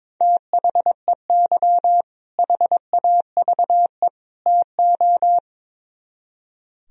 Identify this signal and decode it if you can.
Morse code, heard as a keyed tone, about 22 wpm, 700 Hz. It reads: THEY HAVE TO
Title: Speed-Racing - Sets of 3 Words 22wpm